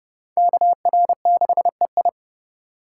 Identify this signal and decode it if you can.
Morse code, heard as a keyed tone, about 35 wpm, 690 Hz. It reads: KR6EI
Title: Speed Racing - USA Call Signs 30wpm